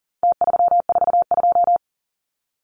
Morse code, heard as a keyed tone, about 40 wpm, 700 Hz. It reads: T342
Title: Numbers - Contest Style - 4-Digit 40wpm